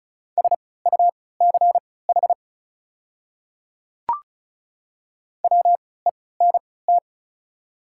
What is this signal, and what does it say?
Morse code, heard as a keyed tone, about 36 wpm, spaced as for 20 wpm, 700 Hz. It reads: SUCH WENT
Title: 3x Character Spacing - Top 200 Words - 35wpm